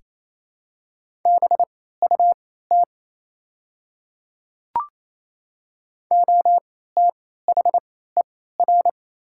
Morse code, heard as a keyed tone, about 27 wpm, spaced as for 16 wpm, 700 Hz. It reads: BUT OTHER
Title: 3x Character Spacing - Top 100 Words - 28wpm